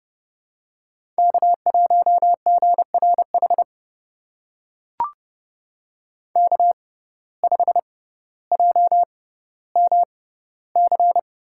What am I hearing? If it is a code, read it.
Morse code, heard as a keyed tone, about 30 wpm, spaced as for 15 wpm, 700 Hz. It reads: K1GRH K5JMC
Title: USA Call Signs - 6x Character Spacing - 30wpm